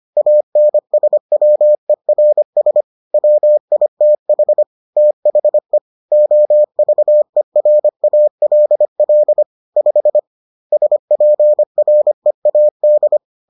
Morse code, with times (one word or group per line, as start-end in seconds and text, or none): 0.2-2.8 ANSWERS
3.1-4.6 WITH
5.0-5.8 THE
6.1-9.4 OVERALL
9.8-10.2 5
10.7-13.2 SPREAD